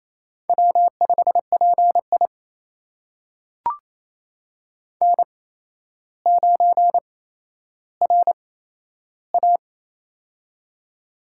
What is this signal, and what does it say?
Morse code, heard as a keyed tone, about 27 wpm, spaced as for 12 wpm, 700 Hz. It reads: W5PI N9RA
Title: USA Call Signs - 4 Characters - 8x Character Spacing - 28wpm